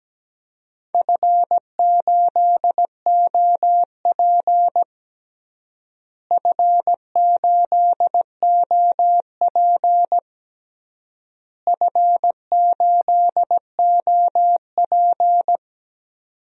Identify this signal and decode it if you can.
Morse code, heard as a keyed tone, about 17 wpm, 700 Hz. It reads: F8OP F8OP F8OP
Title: DXCC-100 Call Signs - 4 Characters - Repeated 3x - 17wpm